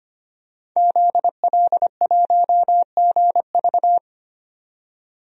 Morse code, heard as a keyed tone, about 25 wpm, 700 Hz. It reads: ZL1GV
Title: DXCC-100 Call Signs - 5 Characters - 25wpm